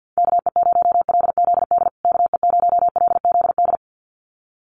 Morse code, heard as a keyed tone, about 50 wpm, 700 Hz. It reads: KE0LZD KE0LZD
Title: USA Call Signs Repeated 2x 50wpm